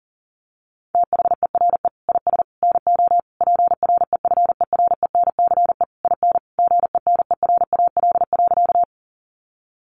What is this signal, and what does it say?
Morse code, heard as a keyed tone, about 37 wpm, 700 Hz. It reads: THERE IS NO PREFERENCE IN GENERAL.
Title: Sentences from Top 1500 Words 40wpm